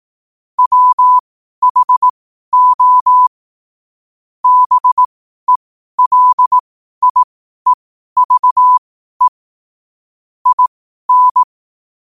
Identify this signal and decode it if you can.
Morse code, heard as a keyed tone, about 18 wpm, spaced as for 12 wpm, 1000 Hz. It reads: WHO BELIEVE IN
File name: qotdfiles / 12wpm